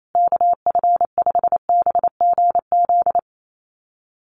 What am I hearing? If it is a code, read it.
Morse code, heard as a keyed tone, about 28 wpm, 690 Hz. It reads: KF5BGZ
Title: USA Call Signs Repeated 3x 28wpm